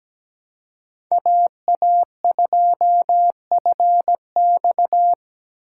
Morse code, heard as a keyed tone, about 17 wpm, 700 Hz. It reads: AA2FX
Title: USA Call Signs - 5 Characters - 6x Character Spacing - 17wpm